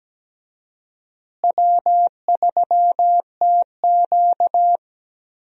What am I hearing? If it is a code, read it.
Morse code, heard as a keyed tone, about 17 wpm, 700 Hz. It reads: W3TQ
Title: USA Call Signs - 4 Characters - Repeated 2x - 17wpm